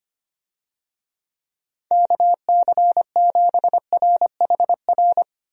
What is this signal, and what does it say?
Morse code, heard as a keyed tone, about 25 wpm, 700 Hz. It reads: KC7RHR
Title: USA Call Signs - 6 Characters - Repeated 3x - 25wpm